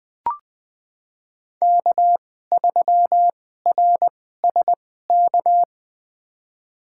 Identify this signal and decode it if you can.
Morse code, heard as a keyed tone, about 20 wpm, spaced as for 14 wpm, 700 Hz. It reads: K3RSK
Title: USA Call Signs - 5 Characters - 2x Character Spacing - 20wpm